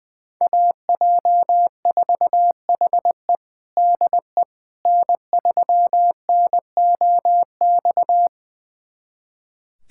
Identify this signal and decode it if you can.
Morse code, heard as a keyed tone, about 20 wpm, 700 Hz. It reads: AJ4HE DE N3NOX
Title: USA Callsign DE Callsign - 5 Characters - 20wpm